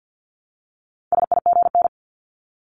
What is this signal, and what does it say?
Morse code, heard as a keyed tone, about 50 wpm, 700 Hz. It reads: SIGN
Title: Multi-letter Phonemes Repeated 3x 50wpm